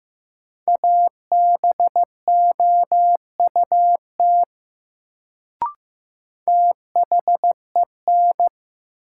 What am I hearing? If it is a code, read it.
Morse code, heard as a keyed tone, about 15 wpm, 700 Hz. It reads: ABOUT THEN
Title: Sentences from Top 100 Words - New Word Drill - 15wpm